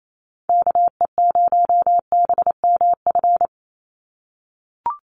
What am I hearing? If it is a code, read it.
Morse code, heard as a keyed tone, about 28 wpm, 700 Hz. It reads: KE0BMF
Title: USA Call Signs 28wpm